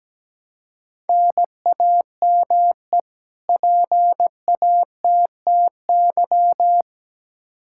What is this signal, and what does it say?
Morse code, heard as a keyed tone, about 17 wpm, 700 Hz. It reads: NAME PATTY
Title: K1USN SST QSO Practice 17wpm